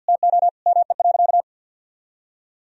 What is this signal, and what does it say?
Morse code, heard as a keyed tone, about 52 wpm, 700 Hz. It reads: TO ME.
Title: Sentences from Top 500 Words and Abbreviations 50wpm